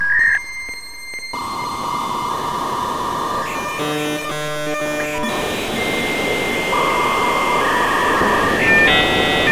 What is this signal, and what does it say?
Dial-up modem handshake.